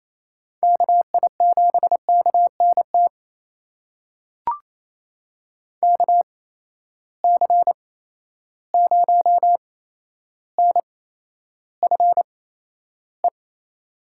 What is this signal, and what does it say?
Morse code, heard as a keyed tone, about 28 wpm, spaced as for 12 wpm, 700 Hz. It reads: KI7KNT KC0NFE
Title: USA Call Signs - 6 Characters - 8x Character Spacing - 28wpm